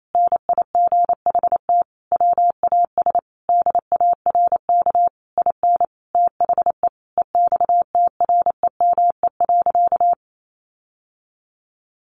Morse code, heard as a keyed tone, about 28 wpm, 700 Hz. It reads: NIGHT WAS DARK IN THE EXTREME.
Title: Sentences from Top 1300 Words 28wpm